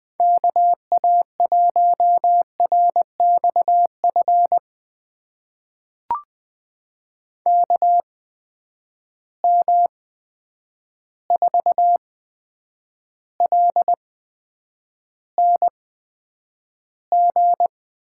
Morse code, heard as a keyed tone, about 20 wpm, spaced as for 9 wpm, 700 Hz. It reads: KA1RXF KM4LNG
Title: USA Call Signs - 6 Characters - 8x Character Spacing - 20wpm